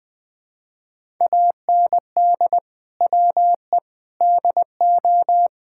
Morse code, heard as a keyed tone, about 20 wpm, 700 Hz. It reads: AND WE DO
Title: Sentences with 4-Word Phrases - Spoken Beginning - 20wpm